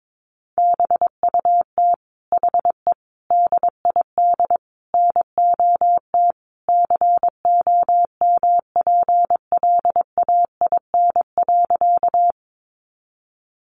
Morse code, heard as a keyed tone, about 22 wpm, 700 Hz. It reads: BUT HE DID NOT COMPLAIN.
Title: Sentences from Top 1300 Words 22wpm